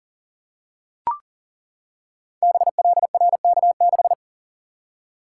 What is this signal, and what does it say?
Morse code, heard as a keyed tone, about 39 wpm, 700 Hz. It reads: BLRK6
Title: Code Groups - 5 Letter:Numbers - 40wpm